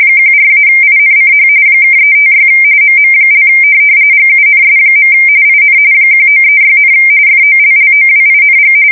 RTTY, about 45.45 baud. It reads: 14-INCH (356 MM) GUNS. IN 1936, FOLLOWING